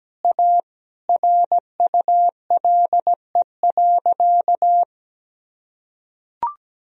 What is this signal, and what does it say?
Morse code, heard as a keyed tone, about 17 wpm, 700 Hz. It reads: A RULE.
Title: Sentences from Top 1000 Words - Review Part 2 - 17wpm